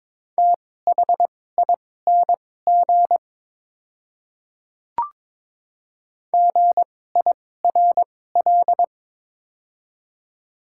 Morse code, heard as a keyed tone, about 21 wpm, spaced as for 16 wpm, 700 Hz. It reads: THING GIRL
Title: 2x Character Spacing - Top 500 Words - 22wpm